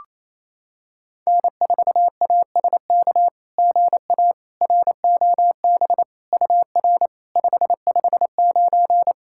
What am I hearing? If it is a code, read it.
Morse code, heard as a keyed tone, about 28 wpm, 700 Hz. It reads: N4ASK GA ROB UR 559
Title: SOTA QSO Practice 28wpm